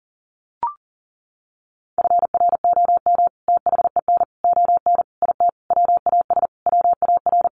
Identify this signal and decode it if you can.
Morse code, heard as a keyed tone, about 37 wpm, 700 Hz. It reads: FROM THEN ON IT WAS WAR
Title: Sentences from Top 400 Words 40wpm